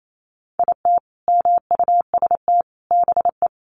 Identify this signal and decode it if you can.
Morse code, heard as a keyed tone, about 28 wpm, 700 Hz. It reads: IT MUST BE
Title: Sentences from Top 500 Words 28wpm